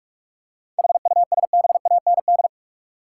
Morse code, heard as a keyed tone, about 46 wpm, 700 Hz. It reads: HUSBAND
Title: Seven Letter Words - 45wpm